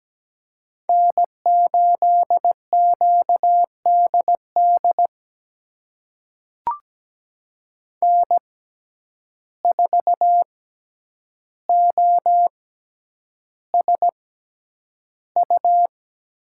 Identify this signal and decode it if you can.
Morse code, heard as a keyed tone, about 17 wpm, spaced as for 9 wpm, 700 Hz. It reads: N8QDD N4OSU
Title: USA Call Signs - 5 Characters - 6x Character Spacing - 17wpm